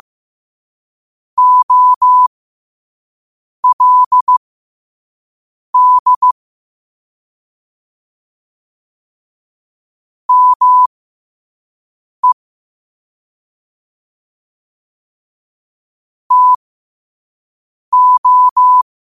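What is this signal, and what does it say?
Morse code, heard as a keyed tone, about 15 wpm, spaced as for 5 wpm, 1000 Hz. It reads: OLD ME TO
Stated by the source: Morse quote-of-the-day practice at 5 wpm.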